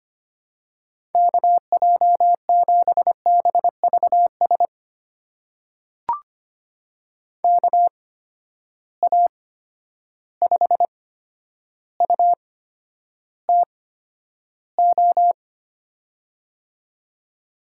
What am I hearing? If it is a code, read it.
Morse code, heard as a keyed tone, about 25 wpm, spaced as for 11 wpm, 700 Hz. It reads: KJ7BVS KA5UTO
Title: USA Call Signs - 6 Characters - 8x Character Spacing - 25wpm